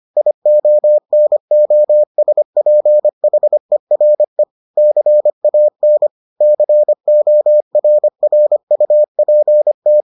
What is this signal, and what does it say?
Morse code, heard as a keyed tone, about 25 wpm, 600 Hz. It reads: IONOSPHERE CAN CORRUPT